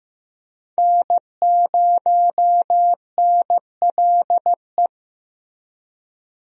Morse code, heard as a keyed tone, about 15 wpm, 700 Hz. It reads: N0NLE
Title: USA Call Signs - 5 Characters - Repeated 3x - 15wpm